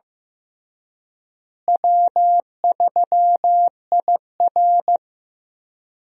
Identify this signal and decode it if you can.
Morse code, heard as a keyed tone, about 15 wpm, 700 Hz. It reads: W3IR
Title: USA Call Signs - 4 Characters - Repeated 3x - 15wpm